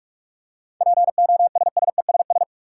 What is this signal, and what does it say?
Morse code, heard as a keyed tone, about 47 wpm, 700 Hz. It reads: POSSESS
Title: Seven Letter Words - 45wpm